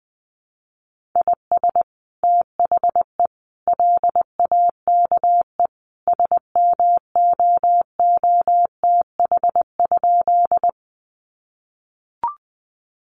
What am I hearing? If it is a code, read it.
Morse code, heard as a keyed tone, about 20 wpm, 700 Hz. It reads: IS THE LAKE SMOOTH?
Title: Sentences from Top 1400 Words 20wpm